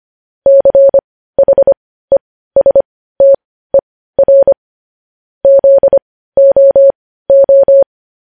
Morse code, heard as a keyed tone, about 25 wpm, spaced as for 15 wpm, 550 Hz. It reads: CHESTER ZOO
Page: news headlines in morse code at 15 wpm